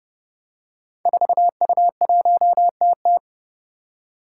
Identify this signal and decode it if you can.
Morse code, heard as a keyed tone, about 30 wpm, 700 Hz. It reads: 4U1TT
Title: Five Challenging Letter-Number Combinations 30wpm